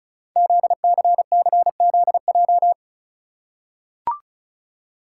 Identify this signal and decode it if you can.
Morse code, heard as a keyed tone, about 33 wpm, 700 Hz. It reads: ZCCZJ
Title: Code Groups - 5 Letters - 35wpm